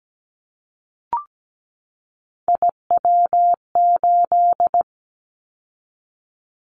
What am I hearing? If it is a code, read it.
Morse code, heard as a keyed tone, about 17 wpm, 700 Hz. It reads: IW8
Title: Callsign Prefixes for Top 25 DXCC - 17wpm